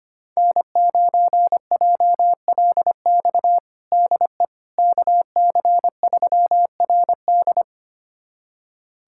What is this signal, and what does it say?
Morse code, heard as a keyed tone, about 25 wpm, 700 Hz. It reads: N9JLX DE KC3RD
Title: USA Callsign DE Callsign 25wpm